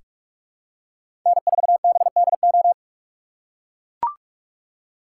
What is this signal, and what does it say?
Morse code, heard as a keyed tone, about 44 wpm, 700 Hz. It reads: N4BDO
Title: USA Call Signs - 4x Character Spacing - 45wpm